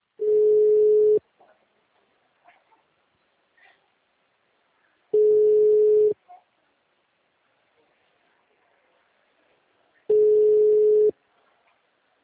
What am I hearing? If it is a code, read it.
Ringback tone (Europe, 425 Hz).